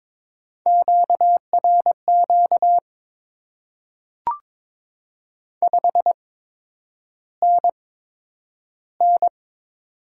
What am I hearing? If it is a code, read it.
Morse code, heard as a keyed tone, about 22 wpm, spaced as for 9 wpm, 700 Hz. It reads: QRQ 5NN
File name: Top 90 QSO Elements - 8x Character Spacing - 22wpm